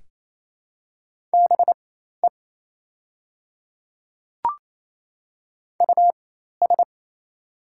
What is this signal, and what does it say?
Morse code, heard as a keyed tone, about 27 wpm, spaced as for 13 wpm, 700 Hz. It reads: BE US
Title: 4x Character Spacing - Top 200 Words - 28wpm